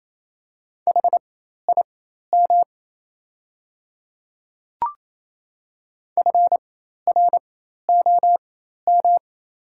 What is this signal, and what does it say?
Morse code, heard as a keyed tone, about 28 wpm, spaced as for 13 wpm, 700 Hz. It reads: HIM FROM
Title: 4x Character Spacing - Top 300 Words - 28wpm